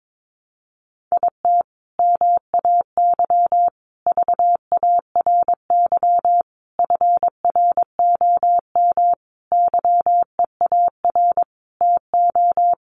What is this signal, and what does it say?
Morse code, heard as a keyed tone, about 22 wpm, 700 Hz. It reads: IT MAY VARY FROM YEAR TO
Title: Sentences from Top 1200 Words 22wpm